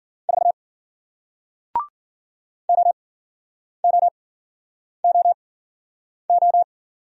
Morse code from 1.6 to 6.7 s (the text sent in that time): OOOO